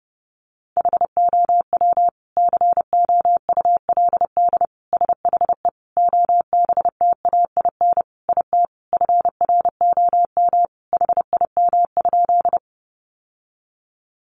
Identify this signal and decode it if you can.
Morse code, heard as a keyed tone, about 30 wpm, 700 Hz. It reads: HOW COULD SHE OBTAIN IT FROM HIM?
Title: Sentences from Top 1300 Words 30wpm